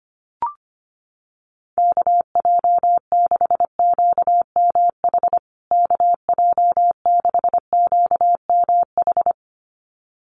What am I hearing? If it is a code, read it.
Morse code, heard as a keyed tone, about 25 wpm, 700 Hz. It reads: KJ6QMH KJ6QMH
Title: USA Call Signs Repeated 2x 25wpm